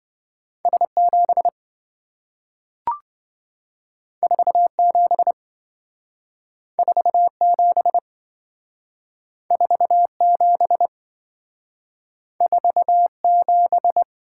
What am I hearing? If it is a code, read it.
Morse code, heard as a keyed tone, about 27 wpm, 700 Hz. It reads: S7 47 47 47 47